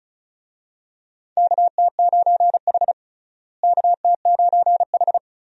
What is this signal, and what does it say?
Morse code, heard as a keyed tone, about 34 wpm, 700 Hz. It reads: KT9H KT9H
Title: USA Call Signs - 4 Characters - Repeated 3x - 35wpm